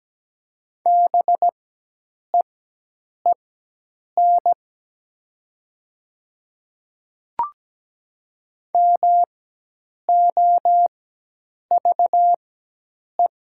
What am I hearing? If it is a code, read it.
Morse code, heard as a keyed tone, about 17 wpm, spaced as for 8 wpm, 700 Hz. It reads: BEEN MOVE
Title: 4x Character Spacing - Top 300 Words - 17wpm